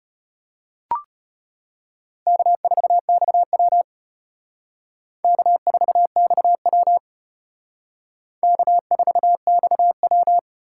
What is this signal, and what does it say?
Morse code, heard as a keyed tone, about 32 wpm, 700 Hz. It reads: K4XW K4XW K4XW